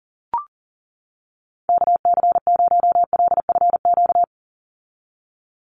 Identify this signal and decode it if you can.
Morse code, heard as a keyed tone, about 40 wpm, 700 Hz. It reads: KC0LFQ
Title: USA Call Signs 40wpm